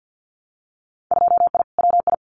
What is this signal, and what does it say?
Morse code, heard as a keyed tone, about 50 wpm, 700 Hz. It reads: 2I WI
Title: Winter Field Day Exchanges 50wpm